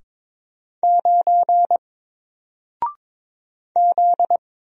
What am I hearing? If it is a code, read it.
Morse code, heard as a keyed tone, about 22 wpm, 700 Hz. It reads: 9 Z
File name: Single-Letter-Number-22wpm